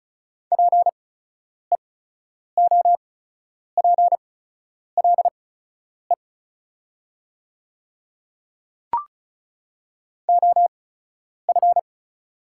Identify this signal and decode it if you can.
Morse code, heard as a keyed tone, about 36 wpm, spaced as for 10 wpm, 700 Hz. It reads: PEOPLE OF
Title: 8x Character Spacing - Top 100 Words - 35wpm